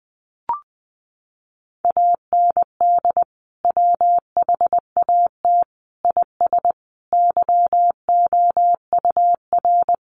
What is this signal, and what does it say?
Morse code, heard as a keyed tone, about 20 wpm, 700 Hz. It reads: AND WHAT IS YOUR
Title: Sentences from Top 900 Words 20wpm